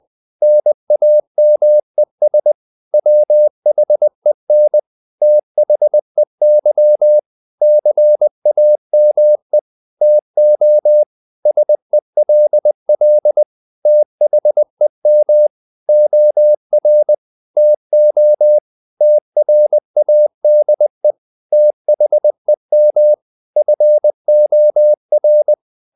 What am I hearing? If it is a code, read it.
Morse code, heard as a keyed tone, about 20 wpm, 600 Hz. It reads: NAMES WHEN THEY CAME TO SELL THEM OR TO TRADE THEM FOR
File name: the old man and the sea / the old man and the sea part 1 20wpm